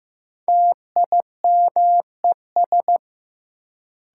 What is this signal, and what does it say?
Morse code, heard as a keyed tone, about 15 wpm, 700 Hz. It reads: TIMES
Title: Sets of 3 Words - New Word Drill - 15wpm